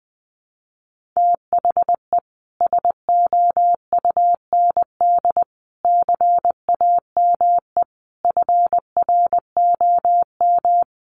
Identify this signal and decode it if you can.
Morse code, heard as a keyed tone, about 20 wpm, 700 Hz. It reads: THE SOUND CAME FROM
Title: Sentences from Top 1400 Words 20wpm